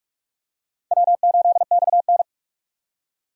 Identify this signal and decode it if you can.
Morse code, heard as a keyed tone, about 47 wpm, 700 Hz. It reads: W8XN
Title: USA Call Signs - 4 Characters - 45wpm